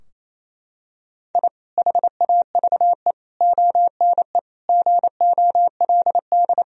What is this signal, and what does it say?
Morse code, heard as a keyed tone, about 28 wpm, 700 Hz. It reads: I HAVE ONE GOLD
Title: Sentences from Top 1000 Words - Review Part 3 - 28wpm